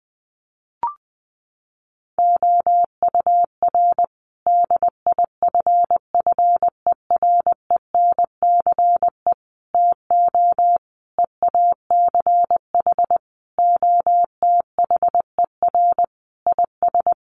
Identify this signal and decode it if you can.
Morse code, heard as a keyed tone, about 20 wpm, 700 Hz. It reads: OUR DIFFERENCE TO EACH OTHER IS
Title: Sentences from Top 1500 Words 20wpm